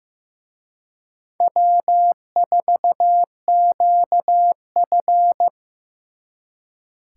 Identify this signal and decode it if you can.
Morse code, heard as a keyed tone, about 18 wpm, 700 Hz. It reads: W4QF
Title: Speed Racing - USA 1x2 Call Signs 15wpm